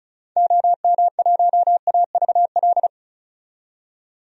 Morse code, heard as a keyed tone, about 36 wpm, 700 Hz. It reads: OM1AVL
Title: DXCC-100 Call Signs - 6 Characters - 35wpm